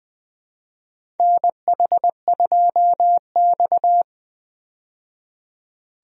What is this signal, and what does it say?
Morse code, heard as a keyed tone, about 21 wpm, 700 Hz. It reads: NH2X
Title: USA Call Signs - 4 Characters - 20wpm